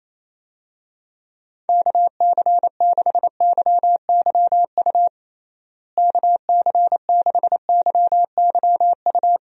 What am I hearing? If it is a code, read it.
Morse code, heard as a keyed tone, about 27 wpm, 700 Hz. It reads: KC6YYU KC6YYU
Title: USA Call Signs - 6 Characters - Repeated 2x - 28wpm